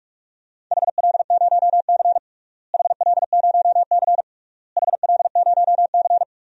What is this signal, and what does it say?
Morse code, heard as a keyed tone, about 46 wpm, 700 Hz. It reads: HL0C HL0C HL0C
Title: DXCC-100 Call Signs - 4 Characters - Repeated 3x - 45wpm